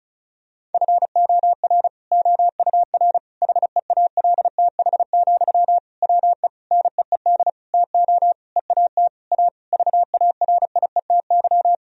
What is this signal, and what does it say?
Morse code, heard as a keyed tone, about 34 wpm, 700 Hz. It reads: FOR OUR HEALTH, WE NEED TO EAT A VARIETY
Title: Sentences from Top 1000 Words - Review Part 4 - 35wpm